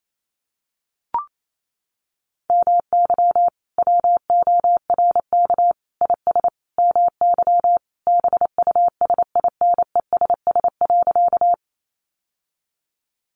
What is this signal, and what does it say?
Morse code, heard as a keyed tone, about 28 wpm, 700 Hz. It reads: MY WORK IS MY BUSINESS.
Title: Sentences from Top 600 Words 28wpm